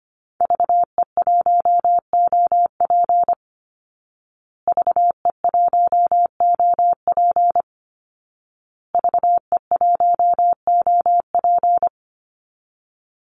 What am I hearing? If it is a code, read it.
Morse code, heard as a keyed tone, about 25 wpm, 700 Hz. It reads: VE1OP VE1OP VE1OP
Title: Call Signs Repeated 2x - 25wpm